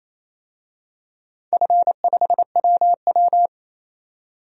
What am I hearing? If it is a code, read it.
Morse code, heard as a keyed tone, about 28 wpm, 700 Hz. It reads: F5WW
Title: DXCC-100 Call Signs - 4 Characters - 28wpm